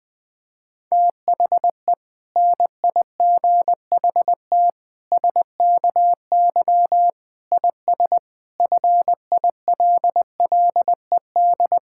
Morse code, heard as a keyed tone, about 21 wpm, 700 Hz. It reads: THE NIGHT SKY IS FILLED
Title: Difficult Sentences with Hints - 20wpm